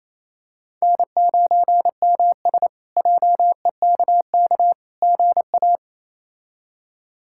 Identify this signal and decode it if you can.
Morse code, heard as a keyed tone, about 28 wpm, 700 Hz. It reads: N9MS JEKK GA
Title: K1USN SST QSO Practice - 28wpm